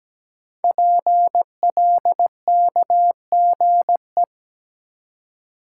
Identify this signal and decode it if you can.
Morse code, heard as a keyed tone, about 17 wpm, 700 Hz. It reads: PLKGE
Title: Code Groups - 5 Letters - 17wpm